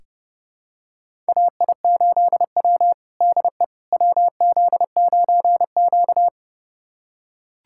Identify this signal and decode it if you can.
Morse code, heard as a keyed tone, about 30 wpm, 700 Hz. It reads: AI8W DE WZ9Q
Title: USA Callsign DE Callsign - 4 Characters - 30wpm